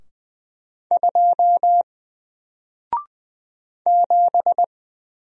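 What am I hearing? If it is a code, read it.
Morse code, heard as a keyed tone, about 20 wpm, 700 Hz. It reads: 2 7
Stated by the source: Single letter and number Morse drill at 20 wpm, from N4LCD.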